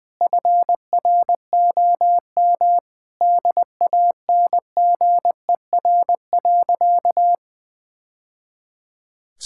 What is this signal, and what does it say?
Morse code, heard as a keyed tone, about 21 wpm, 700 Hz. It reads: FROM DANGER.
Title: Sentences with 2-Word Phrases - Spoken Beginning - 20wpm